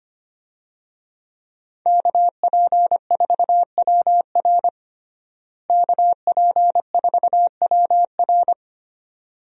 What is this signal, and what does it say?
Morse code, heard as a keyed tone, about 25 wpm, 700 Hz. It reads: KP4WR KP4WR
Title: USA Call Signs - 5 Characters - Repeated 2x - 25wpm